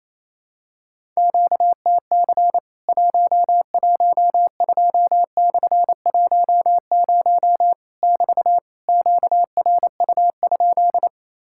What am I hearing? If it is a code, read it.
Morse code, heard as a keyed tone, about 28 wpm, 700 Hz. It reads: QTC 112/10 <BT> QRU?
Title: Worked All Europe DX Contest - QTC Exchange - 28wpm